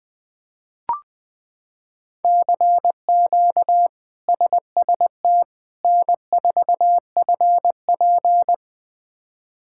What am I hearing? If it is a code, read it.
Morse code, heard as a keyed tone, about 20 wpm, 700 Hz. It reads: CQ SST N4FP
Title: K1USN SST QSO Practice - 20wpm